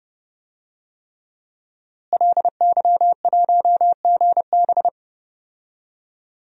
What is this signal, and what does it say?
Morse code, heard as a keyed tone, about 32 wpm, 700 Hz. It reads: LY1GB
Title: DXCC-100 Call Signs - 5 Characters - Repeated 3x - 30wpm